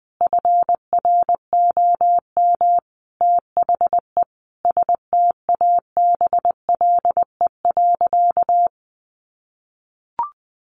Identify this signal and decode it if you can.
Morse code, heard as a keyed tone, about 20 wpm, 700 Hz. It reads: FROM THE STABLE.
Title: Sentences from Top 1500 Words 20wpm